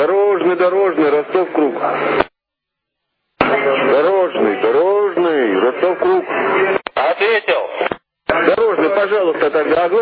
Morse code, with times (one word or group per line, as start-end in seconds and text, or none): none